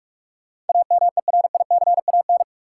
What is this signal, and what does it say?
Morse code, heard as a keyed tone, about 42 wpm, 700 Hz. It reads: AMERICAN
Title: Sets of 2 Words - New Word Drill - 45wpm